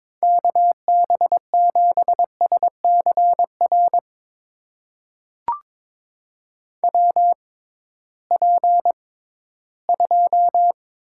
Morse code, heard as a keyed tone, about 22 wpm, spaced as for 14 wpm, 700 Hz. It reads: KB7SCR WP2
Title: USA Call Signs - 6 Characters - 6x Character Spacing - 22wpm